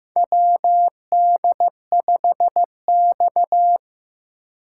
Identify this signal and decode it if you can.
Morse code, heard as a keyed tone, about 15 wpm, 700 Hz. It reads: WD5X
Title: USA Call Signs - 4 Characters - 4x Character Spacing - 15wpm